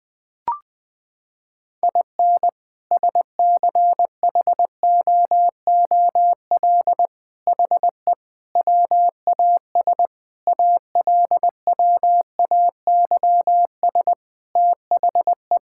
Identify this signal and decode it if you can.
Morse code, heard as a keyed tone, about 21 wpm, 700 Hz. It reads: IN SCHOOL HE WAS ALWAYS THE
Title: Sentences from Top 1000 Words - Review Part 5 - 20wpm